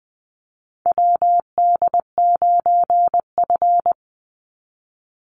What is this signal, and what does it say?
Morse code, heard as a keyed tone, about 20 wpm, 700 Hz. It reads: WD9F
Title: NA SOTA Chaser Callsigns 20wpm